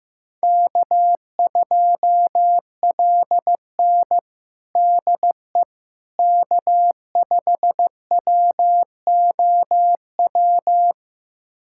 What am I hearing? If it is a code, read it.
Morse code, heard as a keyed tone, about 15 wpm, 700 Hz. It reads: K2LN DE K5WOW
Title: USA Callsign DE Callsign 15wpm